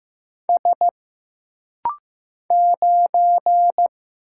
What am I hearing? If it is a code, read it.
Morse code, heard as a keyed tone, about 15 wpm, 700 Hz. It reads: S 9
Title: Single Letter-Number Alt - Rapid-Fire 15wpm